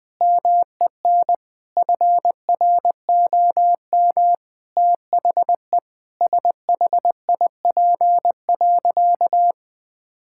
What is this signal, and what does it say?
Morse code, heard as a keyed tone, about 20 wpm, 700 Hz. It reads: MEN FROM THE SHIP.
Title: Sentences from Top 1000 Words - Review Part 5 - 20wpm